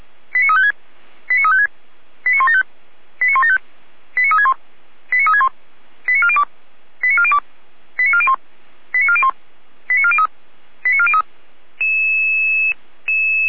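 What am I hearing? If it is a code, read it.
Five-tone selective call (ZVEI), heard as quick runs of five tones, tones of about 70 ms. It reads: 89356 89356 89175 89175 89361 89361 89402 89402 89401 89401 89403 89403
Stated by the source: ZVEI five-tone alert received by SDR.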